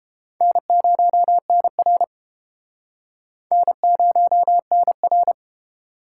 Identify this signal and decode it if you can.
Morse code, heard as a keyed tone, about 32 wpm, 700 Hz. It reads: N0NR N0NR